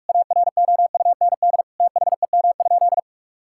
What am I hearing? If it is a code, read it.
Morse code, heard as a keyed tone, about 46 wpm, 700 Hz. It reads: AROUND THEM?
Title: Sentences from Top 1000 Words - Review Part 5 - 45wpm